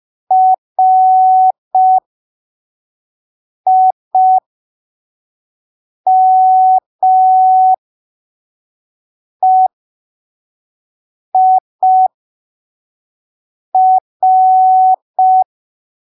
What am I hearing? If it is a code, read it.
Morse code, heard as a keyed tone, about 5 wpm, spaced as for 3 wpm, 750 Hz. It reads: RIMEIR
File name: arquivos / texto / texto 17 05ppm